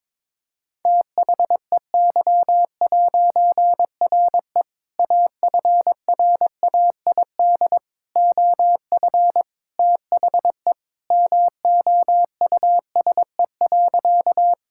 Morse code, heard as a keyed tone, about 22 wpm, 700 Hz. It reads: THEY'RE AFRAID OF THE MOUSE.
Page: morse-code-ninja-practice-22wpm